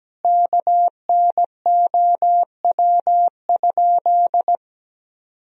Morse code, heard as a keyed tone, about 17 wpm, 700 Hz. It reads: KNOW?
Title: Sentences from Top 200 Words and Abbreviations 17wpm